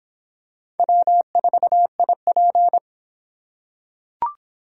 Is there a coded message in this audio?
Yes: Morse code from 0.8 to 2.8 s, heard as a keyed tone, about 24 wpm, 700 Hz. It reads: W4IP